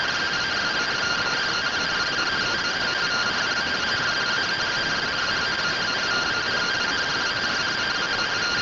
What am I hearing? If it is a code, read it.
Two-tone teleprinter signal, about 170 Hz shift.